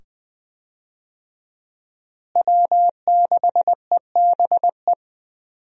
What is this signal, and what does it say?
Morse code, heard as a keyed tone, about 20 wpm, 700 Hz. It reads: W6EBE
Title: USA Call Signs - 5 Characters - Repeated 2x - 20wpm